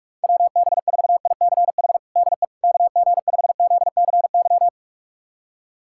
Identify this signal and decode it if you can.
Morse code, heard as a keyed tone, about 46 wpm, 700 Hz. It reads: WB4ICH DE KG5ZCY
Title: USA Callsign DE Callsign - 6 Characters - 45wpm